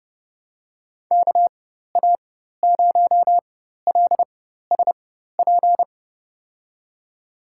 Morse code, heard as a keyed tone, about 30 wpm, spaced as for 14 wpm, 700 Hz. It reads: KA0LSP
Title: USA Call Signs - 6 Characters - 4x Character Spacing - 30wpm